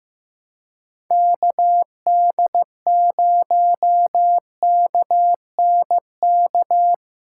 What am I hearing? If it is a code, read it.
Morse code, heard as a keyed tone, about 16 wpm, 700 Hz. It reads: KD0KNK
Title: USA Callsign DE Callsign - 6 Characters - 15wpm